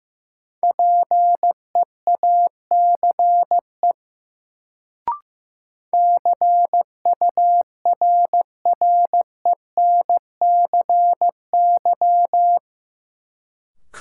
Morse code, heard as a keyed tone, about 15 wpm, 700 Hz. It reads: PEACE CURRENCY
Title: Sentences from Top 1500 Words - New Word Drill - 15wpm